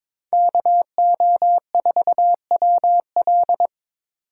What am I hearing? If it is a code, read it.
Morse code, heard as a keyed tone, about 22 wpm, 700 Hz. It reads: KO4WL
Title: USA Call Signs - 5 Characters - 8x Character Spacing - 22wpm